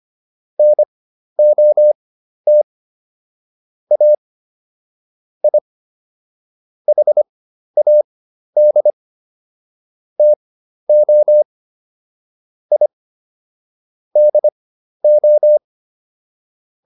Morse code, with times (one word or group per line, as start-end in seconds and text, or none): 0.6-2.6 NOT
3.9-4.1 A
5.4-5.6 I
6.9-8.9 HAD
10.2-11.4 TO
12.7-12.9 I
14.2-15.6 DO